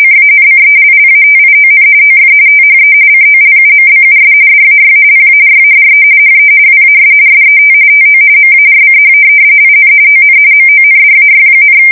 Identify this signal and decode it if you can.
RTTY, about 45.45 baud. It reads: CANCELLATION OF THE COLUMBUS DAY PARADE HE WAS SET TO MARCH IN ON MONDAY,
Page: 170 Shift 2125/2295